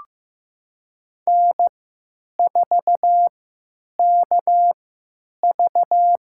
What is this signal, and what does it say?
Morse code, heard as a keyed tone, about 15 wpm, spaced as for 9 wpm, 700 Hz. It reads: N4KV
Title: USA Call Signs - 4 Characters - 3x Character Spacing - 15wpm